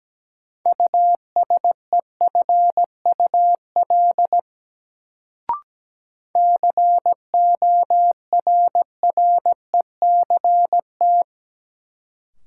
Morse code, heard as a keyed tone, about 17 wpm, 700 Hz. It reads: USEFUL CORRECT
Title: Sentences from Top 700 Words - New Word Drill - 17wpm